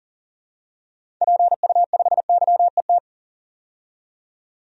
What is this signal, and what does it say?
Morse code, heard as a keyed tone, about 37 wpm, 700 Hz. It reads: PU5YET
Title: DXCC-100 Call Signs - 6 Characters - 40wpm